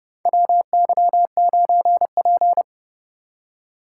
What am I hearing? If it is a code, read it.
Morse code, heard as a keyed tone, about 30 wpm, 700 Hz. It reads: WY9P
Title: USA Call Signs - 4 Characters - 4x Character Spacing - 30wpm